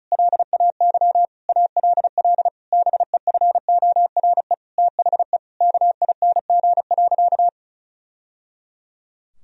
Morse code, heard as a keyed tone, about 34 wpm, 700 Hz. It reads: LAY ALL BEFORE THE KING.
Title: Sentences from Top 1000 Words - Review Part 2 - 35wpm